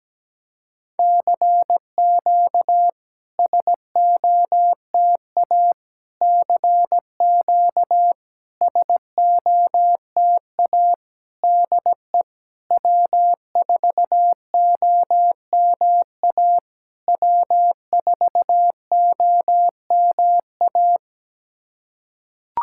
Morse code, heard as a keyed tone, about 17 wpm, 700 Hz. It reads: CQ SOTA CQ SOTA DE W4OMA W4OMA
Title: SOTA QSO Practice 17wpm